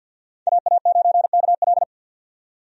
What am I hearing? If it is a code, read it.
Morse code, heard as a keyed tone, about 52 wpm, 700 Hz. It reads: AA9KL
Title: USA Call Signs - 5 Characters - 50wpm